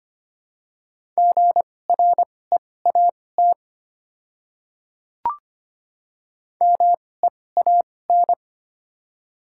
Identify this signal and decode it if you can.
Morse code, heard as a keyed tone, about 25 wpm, spaced as for 18 wpm, 700 Hz. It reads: GREAT MEAN
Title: 2x Character Spacing - Top 300 Words - 25wpm